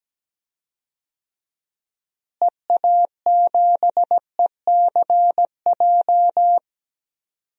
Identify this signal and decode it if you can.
Morse code, heard as a keyed tone, about 17 wpm, 700 Hz. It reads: EA7ECJ
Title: DXCC-100 Call Signs - 6 Characters - Repeated 3x - 17wpm